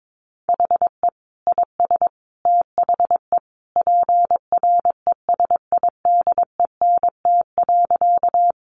Morse code, heard as a keyed tone, about 22 wpm, 700 Hz. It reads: HE IS THE PRESIDENT.
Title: Sentences from Top 1000 Words 22wpm